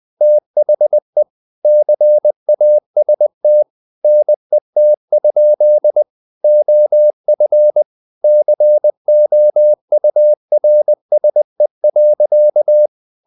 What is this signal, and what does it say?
Morse code, heard as a keyed tone, about 20 wpm, 600 Hz. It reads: THE CAST NET? OF COURSE.
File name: the old man and the sea / the old man and the sea part 1 20wpm